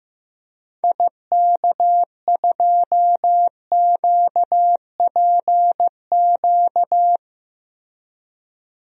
Morse code, heard as a keyed tone, about 15 wpm, 700 Hz. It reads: IK2QPQ
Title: DXCC-100 Call Signs - 6 Characters - Repeated 2x - 15wpm